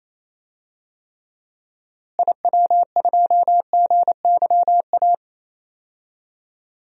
Morse code, heard as a keyed tone, about 28 wpm, 700 Hz. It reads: IW2GYA
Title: DXCC-100 Call Signs - 6 Characters - Repeated 2x - 28wpm